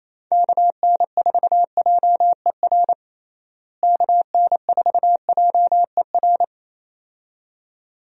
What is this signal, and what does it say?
Morse code, heard as a keyed tone, about 27 wpm, 700 Hz. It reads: KN4JER KN4JER
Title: USA Call Signs - 6 Characters - Repeated 2x - 28wpm